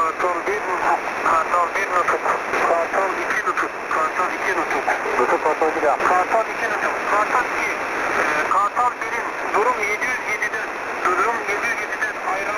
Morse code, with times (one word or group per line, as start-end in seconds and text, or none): none